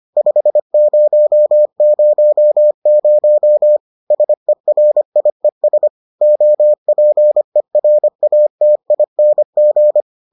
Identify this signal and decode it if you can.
Morse code, heard as a keyed tone, about 25 wpm, 600 Hz. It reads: 5000 SERIES OPERATING